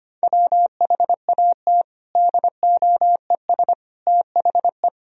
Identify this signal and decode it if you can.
Morse code, heard as a keyed tone, about 25 wpm, 700 Hz. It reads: WHAT DOES THE
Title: Sentences from Top 1000 Words - Review Part 4 - 25wpm